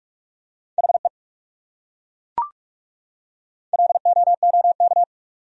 Morse code, heard as a keyed tone, about 41 wpm, 700 Hz. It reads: HE LOOK
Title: Top 100 Words 45wpm